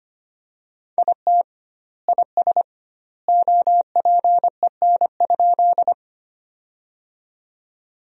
Morse code, heard as a keyed tone, about 25 wpm, 700 Hz. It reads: IT IS OPEN?
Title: 2x Word-Spacing - Sentences from Top 300 Words - 25wpm